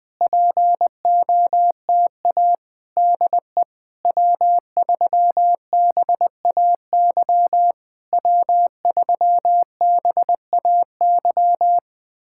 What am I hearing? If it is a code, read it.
Morse code, heard as a keyed tone, about 20 wpm, 690 Hz. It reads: POTA DE W3BAY W3BAY
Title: POTA QSO Practice - 20wpm